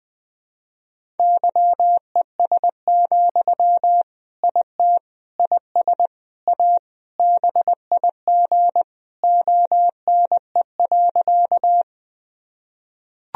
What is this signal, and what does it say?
Morse code, heard as a keyed tone, about 20 wpm, 700 Hz. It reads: YES, IT IS A BIG ONE.
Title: Sentences from Top 1000 Words - Review Part 1 - 20wpm